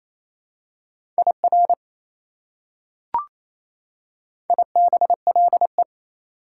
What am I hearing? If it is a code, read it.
Morse code, heard as a keyed tone, about 28 wpm, 700 Hz. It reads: IR IBLE
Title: Prefixes and Suffixes 28wpm